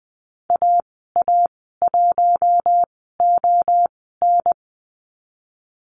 Morse code, heard as a keyed tone, about 20 wpm, spaced as for 14 wpm, 700 Hz. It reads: AA1ON
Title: 082 Lesson 14g - Number 2 - Call Signs 20wpm